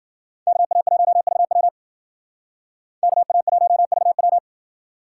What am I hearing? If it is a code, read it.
Morse code, heard as a keyed tone, about 57 wpm, 700 Hz. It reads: KA1VW KA1VW